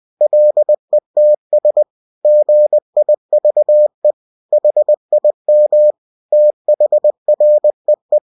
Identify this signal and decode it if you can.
Morse code, heard as a keyed tone, about 20 wpm, 600 Hz. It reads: LETS GIVE HIM THREE
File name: cw-books / winnie-the-pooh / winnie-20wpm-page-0104